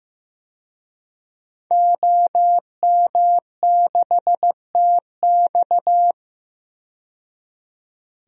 Morse code, heard as a keyed tone, about 15 wpm, 700 Hz. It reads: OM6TX
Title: DXCC-100 Call Signs - 5 Characters - 15wpm